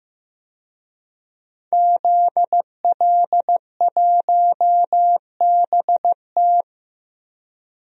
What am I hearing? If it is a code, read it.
Morse code, heard as a keyed tone, about 15 wpm, 700 Hz. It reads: ZL1BT
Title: DXCC-100 Call Signs - 5 Characters - Repeated 2x - 15wpm